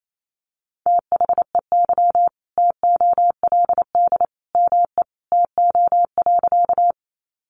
Morse code, heard as a keyed tone, about 28 wpm, 700 Hz. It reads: THEY TOLD ME TO.
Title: Sentences from Top 400 Words 28wpm